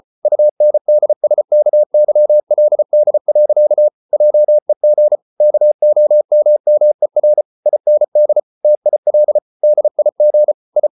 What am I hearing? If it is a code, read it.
Morse code, heard as a keyed tone, about 34 wpm, 600 Hz. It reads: UNDSKYLD. JEG KOMMER IND TIL DIG I
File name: Ferie QRQ 34wpm